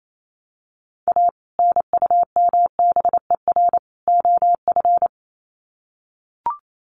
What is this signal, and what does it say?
Morse code, heard as a keyed tone, about 28 wpm, 700 Hz. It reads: A NUMBER OF
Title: Sets of 3 Words 28wpm